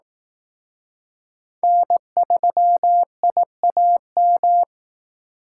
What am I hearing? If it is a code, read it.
Morse code, heard as a keyed tone, about 19 wpm, 700 Hz. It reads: N3IAM